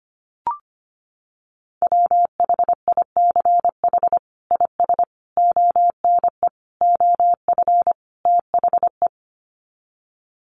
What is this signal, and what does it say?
Morse code, heard as a keyed tone, about 25 wpm, 700 Hz. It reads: WHICH IS ONE OF THE
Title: Sets of 5 Words Encore 25wpm